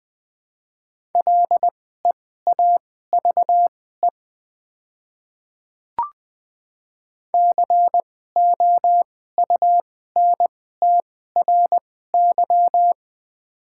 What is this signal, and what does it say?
Morse code, heard as a keyed tone, about 20 wpm, spaced as for 14 wpm, 700 Hz. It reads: LEAVE COUNTRY
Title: 2x Character Spacing - Top 500 Words - 20wpm